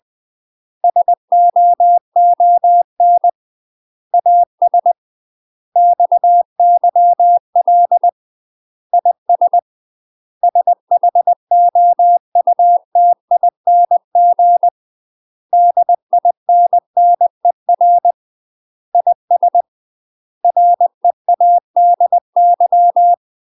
Morse code, heard as a keyed tone, about 20 wpm, 700 Hz. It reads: SOON AS XYL IS SHOUTING DINNER IS READY
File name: UK MASTER QSOs TRAINING / 20WPM / UK MASTER QSOs 0065